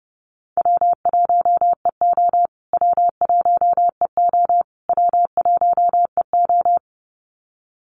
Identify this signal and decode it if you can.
Morse code, heard as a keyed tone, about 30 wpm, 700 Hz. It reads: W1EO W1EO W1EO
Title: USA Call Signs Repeated 3x 30wpm